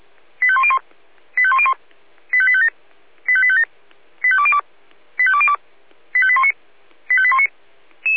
Five-tone selective call (ZVEI), heard as quick runs of five tones, tones of about 70 ms. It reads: 85101 85101 85857 85857 85202 85202 85819 85819